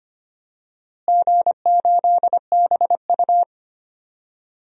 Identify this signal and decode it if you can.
Morse code, heard as a keyed tone, about 25 wpm, 700 Hz. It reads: G8BU
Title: DXCC-100 Call Signs - 4 Characters - Repeated 3x - 25wpm